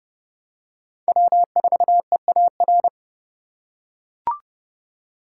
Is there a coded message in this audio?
Yes: Morse code from 1.1 to 2.9 s, heard as a keyed tone, about 30 wpm, 700 Hz. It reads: W4EAR